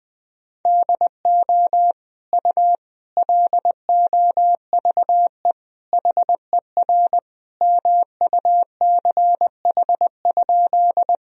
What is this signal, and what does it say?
Morse code, heard as a keyed tone, about 20 wpm, 700 Hz. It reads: DO U LOVE HER MUCH?
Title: Sentences from Top 400 Words and Abbreviations 20wpm